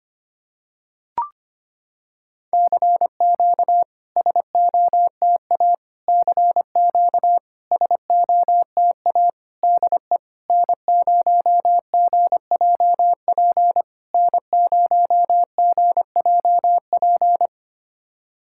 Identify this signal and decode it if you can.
Morse code, heard as a keyed tone, about 25 wpm, 690 Hz. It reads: CQ SOTA CQ SOTA DE N0GJP N0GJP
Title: SOTA QSO Practice 25wpm